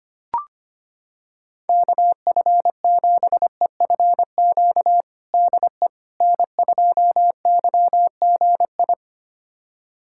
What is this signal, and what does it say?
Morse code, heard as a keyed tone, about 25 wpm, 700 Hz. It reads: KF7EFQ DE N2YGI
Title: USA Callsign DE Callsign 25wpm